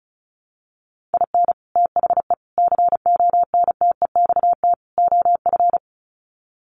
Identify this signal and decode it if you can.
Morse code, heard as a keyed tone, about 34 wpm, 700 Hz. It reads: IN THE CONTEXT OF
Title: Sets of 4 Words 35wpm